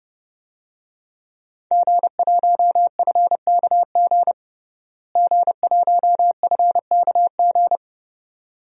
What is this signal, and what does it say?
Morse code, heard as a keyed tone, about 32 wpm, 700 Hz. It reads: G1FKG G1FKG
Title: DXCC-100 Call Signs - 5 Characters - Repeated 2x - 30wpm